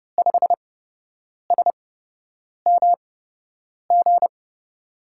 Morse code, heard as a keyed tone, about 31 wpm, spaced as for 8 wpm, 700 Hz. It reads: 5SMG
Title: USA Call Signs - 5 Characters - 8x Character Spacing - 30wpm